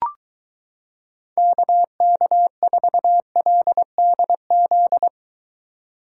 Morse code, heard as a keyed tone, about 23 wpm, 700 Hz. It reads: KK4LDZ